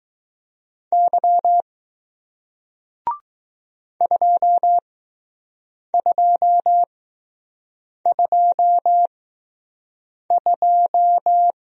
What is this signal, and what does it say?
Morse code, heard as a keyed tone, about 21 wpm, spaced as for 6 wpm, 700 Hz. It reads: Y 2222